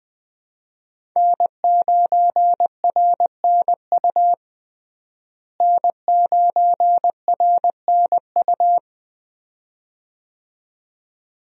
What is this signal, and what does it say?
Morse code, heard as a keyed tone, about 21 wpm, 700 Hz. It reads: N9RNU N9RNU
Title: USA Call Signs - 5 Characters - Repeated 2x - 20wpm